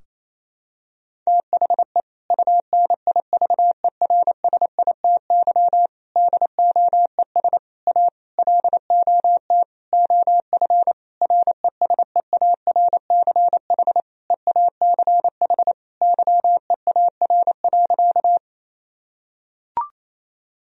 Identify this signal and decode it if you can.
Morse code, heard as a keyed tone, about 28 wpm, 700 Hz. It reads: THE UNIVERSITY DOES A LOT OF RESEARCH EACH YEAR.
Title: Sentences from Top 1000 Words - Review Part 4 - 28wpm